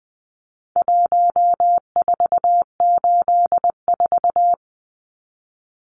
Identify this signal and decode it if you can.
Morse code, heard as a keyed tone, about 20 wpm, 700 Hz. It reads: 1484
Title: Numbers - Contest Style - 4-Digit 20wpm